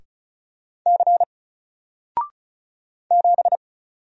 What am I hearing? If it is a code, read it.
Morse code, heard as a keyed tone, about 36 wpm, 700 Hz. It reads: C 7
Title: Single Character (Letter:Number:Punctuation:Prosign) 35wpm